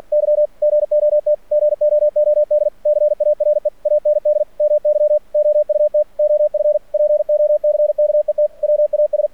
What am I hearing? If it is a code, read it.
Morse code, heard as a keyed tone, about 48 wpm, 600 Hz. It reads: Y GOT GOOD CARE AND MY OUT OF POCKET WAS
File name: archived / CRSnet